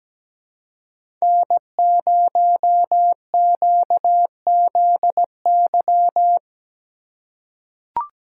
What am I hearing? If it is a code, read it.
Morse code, heard as a keyed tone, about 17 wpm, 700 Hz. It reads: N0QZY
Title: USA Call Signs - 3x Character Spacing - 17wpm